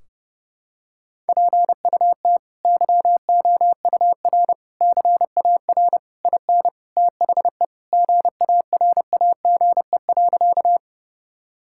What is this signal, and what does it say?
Morse code, heard as a keyed tone, about 32 wpm, 700 Hz. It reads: PUT YOUR CAR IN THE GARAGE.
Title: Sentences from Top 1600 Words 30wpm